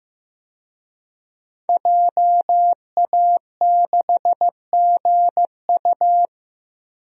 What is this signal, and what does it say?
Morse code, heard as a keyed tone, about 15 wpm, 700 Hz. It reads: JA6GU
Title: DXCC-100 Call Signs - 5 Characters - 15wpm